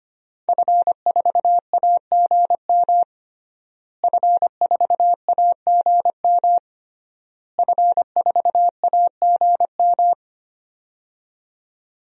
Morse code, heard as a keyed tone, about 25 wpm, 700 Hz. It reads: F4AGM F4AGM F4AGM
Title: DXCC-100 Call Signs - 5 Characters - Repeated 3x - 25wpm